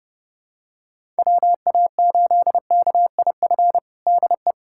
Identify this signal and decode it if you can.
Morse code, heard as a keyed tone, about 30 wpm, 700 Hz. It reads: WA8KIF DE
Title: USA Callsign DE Callsign - 6 Characters - 30wpm